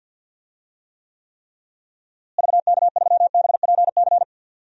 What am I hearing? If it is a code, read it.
Morse code, heard as a keyed tone, about 51 wpm, 700 Hz. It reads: VK3BPC
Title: DXCC-100 Call Signs - 6 Characters - Repeated 3x - 50wpm